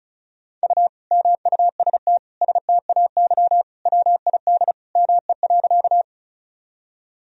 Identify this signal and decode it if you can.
Morse code, heard as a keyed tone, about 36 wpm, 700 Hz. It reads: U MUST STAY WID ME.
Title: Sentences from Top 500 Words and Abbreviations 35wpm